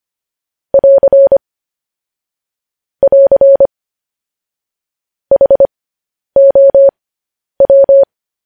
Morse code, heard as a keyed tone, about 25 wpm, spaced as for 10 wpm, 550 Hz. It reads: <AR> <AR> HOW
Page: news headlines in morse code at 10 wpm